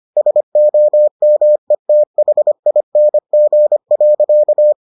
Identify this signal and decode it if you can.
Morse code, heard as a keyed tone, about 25 wpm, 600 Hz. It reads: SOMETHING.